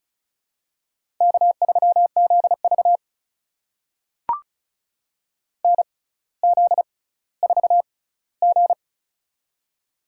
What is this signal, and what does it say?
Morse code, heard as a keyed tone, about 36 wpm, spaced as for 18 wpm, 700 Hz. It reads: K3ZV NZ4G
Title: USA Call Signs - 4 Characters - 6x Character Spacing - 35wpm